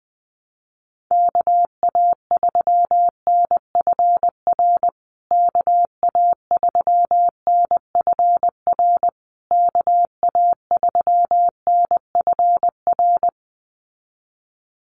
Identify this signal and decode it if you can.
Morse code, heard as a keyed tone, about 20 wpm, 700 Hz. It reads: KA3NFR KA3NFR KA3NFR
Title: USA Call Signs Repeated 3x 20wpm